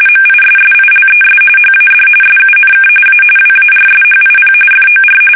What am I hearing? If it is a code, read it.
RTTY, about 45.45 baud. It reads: MISSILES HAD SUNK A WARSHIP FOR
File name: fsk850 / battleshipnj